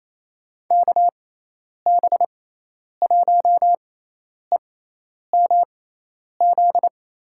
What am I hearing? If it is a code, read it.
Morse code, heard as a keyed tone, about 28 wpm, spaced as for 10 wpm, 700 Hz. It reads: KB1EMZ
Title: USA Call Signs - 6 Characters - 6x Character Spacing - 28wpm